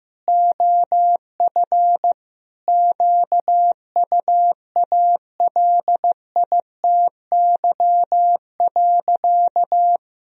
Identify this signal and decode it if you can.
Morse code, heard as a keyed tone, about 15 wpm, 700 Hz. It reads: OF QUALITY.
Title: Sentences from Top 1000 Words - Review Part 4 - 15wpm